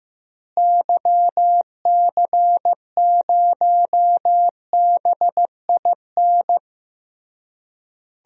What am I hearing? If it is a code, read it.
Morse code, heard as a keyed tone, about 15 wpm, 700 Hz. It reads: YC0BIN
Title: DXCC-100 Call Signs - 6 Characters - Repeated 3x - 15wpm